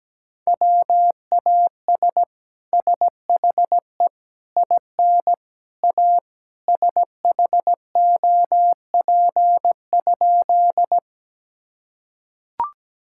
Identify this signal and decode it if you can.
Morse code, heard as a keyed tone, about 17 wpm, 700 Hz. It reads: WAS SHE IN A SHOP?
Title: Sentences from Top 1600 Words 17wpm